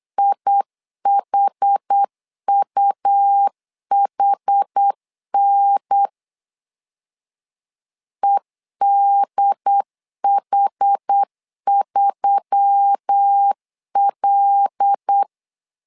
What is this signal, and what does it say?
Morse code, heard as a keyed tone, about 8 wpm, 800 Hz. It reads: IHUHN EDH3L
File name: LEZIONI / 5 LBVC / 5 L 8wpm 2831